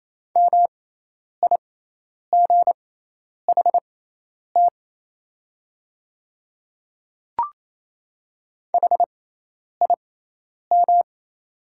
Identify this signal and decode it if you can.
Morse code, heard as a keyed tone, about 28 wpm, spaced as for 10 wpm, 700 Hz. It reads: MIGHT HIM
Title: 6x Character Spacing - Top 500 Words - 28wpm